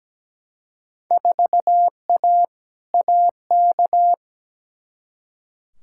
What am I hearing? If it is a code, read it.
Morse code, heard as a keyed tone, about 17 wpm, 700 Hz. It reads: 4A AK
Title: Field Day Exchanges Encore - 17wpm